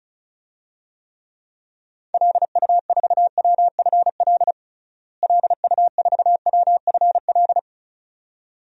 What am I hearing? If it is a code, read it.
Morse code, heard as a keyed tone, about 36 wpm, 700 Hz. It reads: LU4WFL LU4WFL
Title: DXCC-100 Call Signs - 6 Characters - Repeated 2x - 35wpm